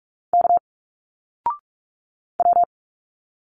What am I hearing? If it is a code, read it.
Morse code, heard as a keyed tone, about 45 wpm, 700 Hz. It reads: K W
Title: Single Letters 45wpm